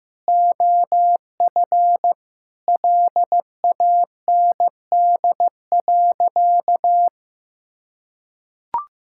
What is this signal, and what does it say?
Morse code, heard as a keyed tone, about 15 wpm, 700 Hz. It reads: OF LAND.
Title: Sentences from Top 1600 Words 15wpm